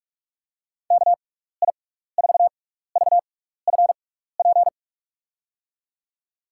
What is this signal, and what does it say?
Morse code, heard as a keyed tone, about 46 wpm, spaced as for 15 wpm, 700 Hz. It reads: KI4VFP
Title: USA Call Signs - 6 Characters - 6x Character Spacing - 45wpm